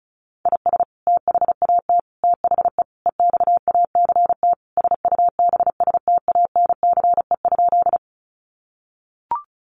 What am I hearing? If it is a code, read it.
Morse code, heard as a keyed tone, about 34 wpm, 700 Hz. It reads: IS THAT THE EXACT SUBSTANCE?
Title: Sentences from Top 1500 Words 35wpm